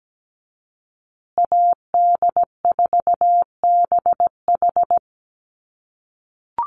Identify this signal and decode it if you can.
Morse code, heard as a keyed tone, about 17 wpm, 700 Hz. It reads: AD4BH
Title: USA Call Signs Repeated 2x 17wpm